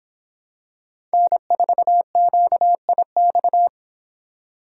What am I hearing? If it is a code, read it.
Morse code, heard as a keyed tone, about 25 wpm, 700 Hz. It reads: N4QIX